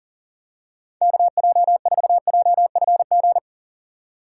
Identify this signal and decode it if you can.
Morse code, heard as a keyed tone, about 40 wpm, 700 Hz. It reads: KJ4JFG
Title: USA Call Signs - 6x Character Spacing - 40wpm